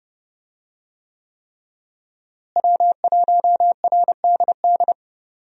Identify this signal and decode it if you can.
Morse code, heard as a keyed tone, about 30 wpm, 700 Hz. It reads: W1RDD
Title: USA Call Signs - 5 Characters - Repeated 3x - 30wpm